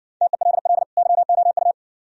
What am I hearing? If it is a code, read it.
Morse code, heard as a keyed tone, about 57 wpm, 700 Hz. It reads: TELL YOU